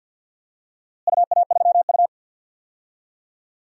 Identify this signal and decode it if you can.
Morse code, heard as a keyed tone, about 52 wpm, 700 Hz. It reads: UA3U
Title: DXCC-100 Call Signs - 4 Characters - Repeated 2x - 50wpm